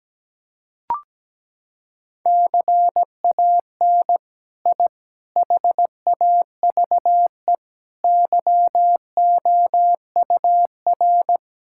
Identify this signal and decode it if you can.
Morse code, heard as a keyed tone, about 17 wpm, 700 Hz. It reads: CAN I HAVE YOUR
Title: Sentences from Top 1000 Words - Review Part 5 - 17wpm